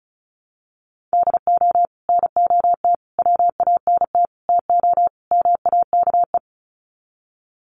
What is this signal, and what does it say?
Morse code, heard as a keyed tone, about 34 wpm, 700 Hz. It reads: DO NOT WANT TO MAKE
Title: Sets of 5 Words Encore 35wpm